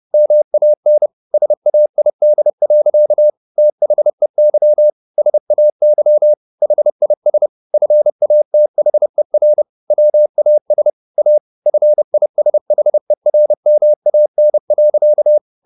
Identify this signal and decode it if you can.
Morse code, heard as a keyed tone, about 30 wpm, 600 Hz. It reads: MAN SAID. THEY SAY HIS FATHER WAS A FISHERMAN.
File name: the old man and the sea / the old man and the sea 1 30wpm